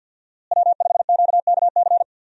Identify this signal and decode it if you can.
Morse code, heard as a keyed tone, about 49 wpm, 700 Hz. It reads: W5QKC
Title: USA Call Signs - 5 Characters - Repeated 3x - 50wpm